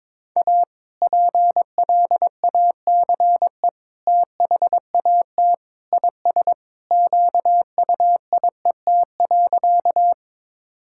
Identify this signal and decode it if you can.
Morse code, heard as a keyed tone, about 21 wpm, 700 Hz. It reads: A PLACE THAT IS QUIET.
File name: practice / 22wpm / Sentences with 2-Word Phrases - 22wpm